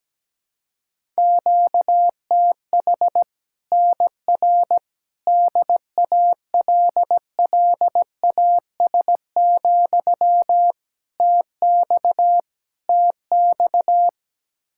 Morse code, heard as a keyed tone, about 17 wpm, 700 Hz. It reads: QTH NR DALLAS, TX TX
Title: Practice QSO - Exchange 1 - Beginning - 17wpm